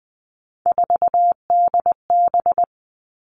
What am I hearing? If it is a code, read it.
Morse code, heard as a keyed tone, about 20 wpm, 700 Hz. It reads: 4DB
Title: All Cut Numbers - 3-Digit 20wpm